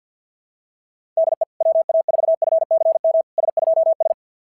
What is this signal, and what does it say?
Morse code, heard as a keyed tone, about 49 wpm, 650 Hz. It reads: DE WA4FCM S2S
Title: SOTA QSO Practice 50wpm